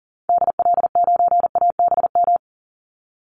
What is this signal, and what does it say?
Morse code, heard as a keyed tone, about 40 wpm, 700 Hz. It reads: DL9ABM
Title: Call Signs - 40wpm